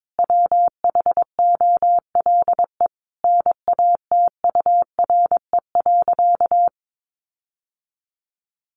Morse code, heard as a keyed tone, about 22 wpm, 700 Hz. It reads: WHOLE NATURE.
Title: Sentences from Top 1500 Words 22wpm